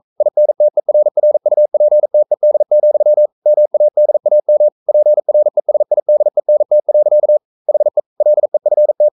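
Morse code, heard as a keyed tone, about 42 wpm, 600 Hz. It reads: INTERRUPTED, MADAM PRESIDENT. HE LEFT
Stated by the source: Morse code sending of an English novel at 42 wpm.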